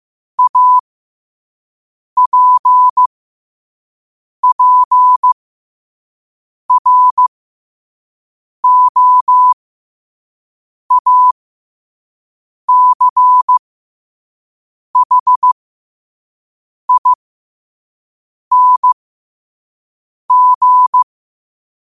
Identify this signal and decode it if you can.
Morse code, heard as a keyed tone, about 15 wpm, spaced as for 5 wpm, 1000 Hz. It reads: APPROACHING
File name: qotdfiles / 5wpm